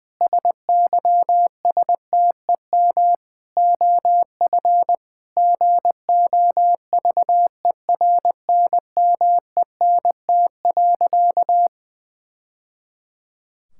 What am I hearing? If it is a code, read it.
Morse code, heard as a keyed tone, about 20 wpm, 700 Hz. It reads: SYSTEM OF GOVERNMENT.
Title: Sentences from Top 1000 Words - Review Part 6 - 20wpm